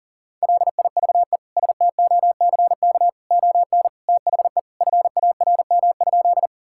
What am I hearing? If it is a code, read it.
Morse code, heard as a keyed tone, about 40 wpm, 700 Hz. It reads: LIVE STOCK ON THE FARM?
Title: Sentences from Top 1000 Words - Review Part 5 - 40wpm